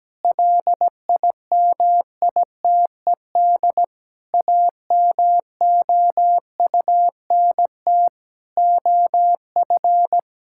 Morse code, heard as a keyed tone, about 17 wpm, 700 Hz. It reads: LIMITED AMOUNT OF
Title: Sentences from Top 1000 Words - Review Part 6 - 17wpm